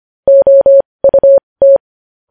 Morse code, heard as a keyed tone, about 25 wpm, spaced as for 20 wpm, 550 Hz. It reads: OUT
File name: podcast / News headlines-1x-25x20